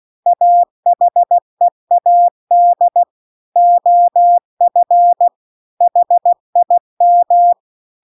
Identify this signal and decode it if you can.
Morse code, heard as a keyed tone, about 16 wpm, 700 Hz. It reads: AHEAD OF HIM
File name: CW-The-Big-Twoheaded-River-2-16WPM